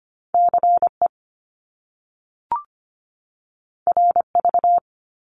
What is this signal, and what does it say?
Morse code, heard as a keyed tone, about 25 wpm, 700 Hz. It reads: CE RV
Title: Top Two Letter Combinations in Words 25wpm